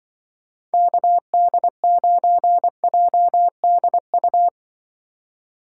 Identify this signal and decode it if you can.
Morse code, heard as a keyed tone, about 25 wpm, 700 Hz. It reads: KD9JDU